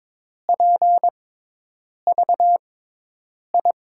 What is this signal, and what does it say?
Morse code, heard as a keyed tone, about 22 wpm, spaced as for 8 wpm, 700 Hz. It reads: PVI
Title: USA Call Signs - 6 Characters - 6x Character Spacing - 22wpm